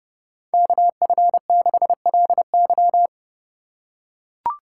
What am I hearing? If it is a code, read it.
Morse code, heard as a keyed tone, about 30 wpm, 700 Hz. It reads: KF6LY
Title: USA Call Signs - 5 Characters - 4x Character Spacing - 30wpm